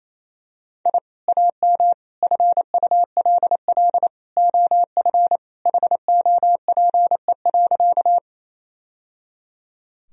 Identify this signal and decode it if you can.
Morse code, heard as a keyed tone, about 27 wpm, 700 Hz. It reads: I AM FULL OF HOPE.
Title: Sentences from Top 1000 Words - Review Part 3 - 28wpm